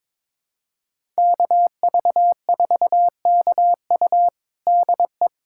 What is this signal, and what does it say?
Morse code, heard as a keyed tone, about 22 wpm, 700 Hz. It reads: KV4KU DE
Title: USA Callsign DE Callsign 22wpm